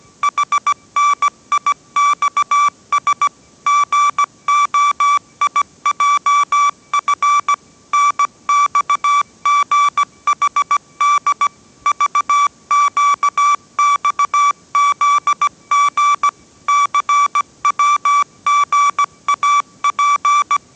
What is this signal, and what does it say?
Morse code, heard as a keyed tone, about 17 wpm, 1160 Hz. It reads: HNIXS GOIJF NXGHD VQXZG CWGAP
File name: 18wpm